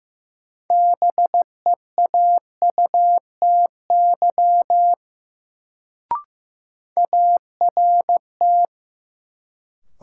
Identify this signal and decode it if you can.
Morse code, heard as a keyed tone, about 15 wpm, 700 Hz. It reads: BEAUTY ART
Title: Sentences from Top 600 Words - New Word Drill - 15wpm